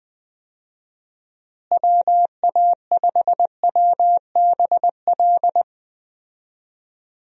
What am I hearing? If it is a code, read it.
Morse code, heard as a keyed tone, about 21 wpm, 700 Hz. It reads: WA5WBL
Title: USA Call Signs - 6 Characters - Repeated 3x - 20wpm